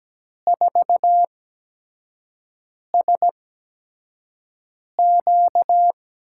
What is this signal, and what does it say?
Morse code, heard as a keyed tone, about 17 wpm, spaced as for 5 wpm, 700 Hz. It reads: 4SQ
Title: USA Call Signs - 6 Characters - 8x Character Spacing - 17wpm